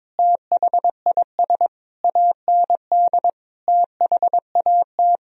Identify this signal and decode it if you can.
Morse code, heard as a keyed tone, about 22 wpm, 700 Hz. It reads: THIS AND THAT
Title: Sentences from Top 1000 Words - Review Part 1 - 22wpm